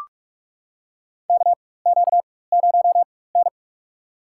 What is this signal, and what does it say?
Morse code, heard as a keyed tone, about 43 wpm, spaced as for 21 wpm, 700 Hz. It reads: KQ0N
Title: USA Call Signs - 4 Characters - 4x Character Spacing - 45wpm